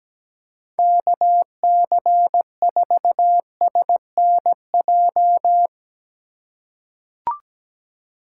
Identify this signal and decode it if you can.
Morse code, heard as a keyed tone, about 17 wpm, 700 Hz. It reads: KC4SNJ
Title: USA Call Signs - 6 Characters - 4x Character Spacing - 17wpm